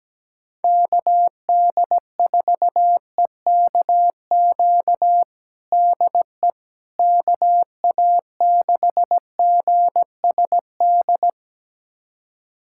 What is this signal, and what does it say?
Morse code, heard as a keyed tone, about 17 wpm, 700 Hz. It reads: KD4EKQ DE KA6GSD
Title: USA Callsign DE Callsign - 6 Characters - 17wpm